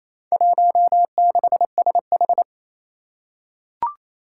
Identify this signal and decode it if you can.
Morse code, heard as a keyed tone, about 28 wpm, 700 Hz. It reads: 16SH
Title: Four Challenging Letter-Number Combinations 28wpm